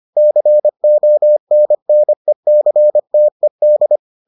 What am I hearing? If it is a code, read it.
Morse code, heard as a keyed tone, about 25 wpm, 600 Hz. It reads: CONNECTED